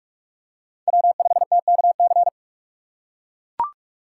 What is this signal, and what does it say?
Morse code, heard as a keyed tone, about 43 wpm, 700 Hz. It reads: W5TKC
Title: USA Call Signs - 5 Characters - 3x Character Spacing - 45wpm